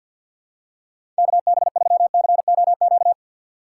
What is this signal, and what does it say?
Morse code, heard as a keyed tone, about 50 wpm, 700 Hz. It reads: KB3COQ
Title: USA Call Signs - 6 Characters - 6x Character Spacing - 50wpm